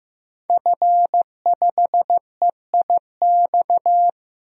Morse code, heard as a keyed tone, about 15 wpm, 700 Hz. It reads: F5EIX
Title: DXCC-100 Call Signs - 5 Characters - Repeated 2x - 15wpm